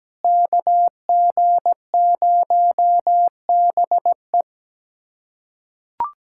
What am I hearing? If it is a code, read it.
Morse code, heard as a keyed tone, about 17 wpm, 700 Hz. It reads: KG0BE
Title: USA Callsign DE Callsign 17wpm